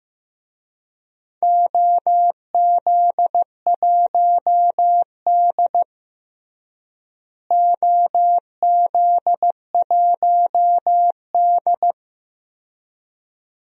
Morse code, heard as a keyed tone, about 15 wpm, 700 Hz. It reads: OZ1D OZ1D
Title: DXCC-100 Call Signs - 4 Characters - Repeated 2x - 15wpm